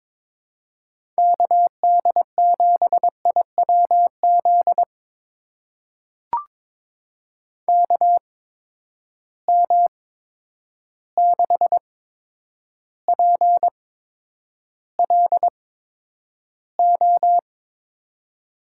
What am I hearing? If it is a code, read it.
Morse code, heard as a keyed tone, about 21 wpm, spaced as for 9 wpm, 700 Hz. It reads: KD7IWZ KM6PLO
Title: USA Call Signs - 6 Characters - 8x Character Spacing - 22wpm